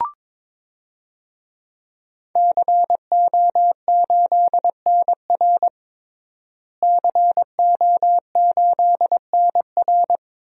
Morse code, heard as a keyed tone, about 21 wpm, 700 Hz. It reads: CO8NR CO8NR
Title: DXCC-100 Call Signs - 5 Characters - Repeated 3x - 22wpm